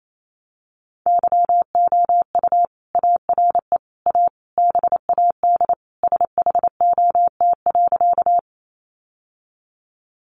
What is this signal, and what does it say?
Morse code, heard as a keyed tone, about 28 wpm, 700 Hz. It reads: YOU ARE A BAD SHOT.
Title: Sentences from Top 1100 Words 28wpm